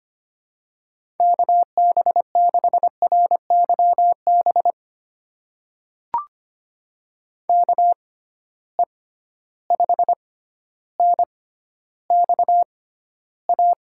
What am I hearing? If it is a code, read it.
Morse code, heard as a keyed tone, about 25 wpm, spaced as for 13 wpm, 700 Hz. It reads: KB6RYB KE5NXA
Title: USA Call Signs - 6 Characters - 6x Character Spacing - 25wpm